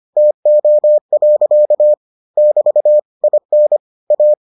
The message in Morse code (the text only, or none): TO. <BT> IN A